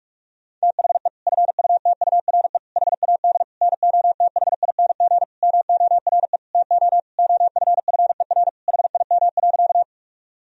Morse code, heard as a keyed tone, about 42 wpm, 700 Hz. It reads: THE FUTURE HAD NOTHING MORE TO OFFER HIM.
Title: Sentences from Top 1000 Words - Review Part 4 - 45wpm